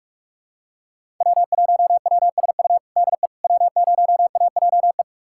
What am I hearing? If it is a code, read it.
Morse code, heard as a keyed tone, about 41 wpm, 700 Hz. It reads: W1WSU DE W0AJE
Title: USA Callsign DE Callsign - 5 Characters - 45wpm